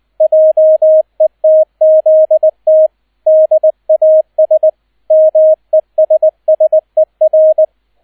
Morse code, heard as a keyed tone, about 20 wpm, 630 Hz, with radio noise. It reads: JETZT DAS MESSER
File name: Erster Streich 20wpm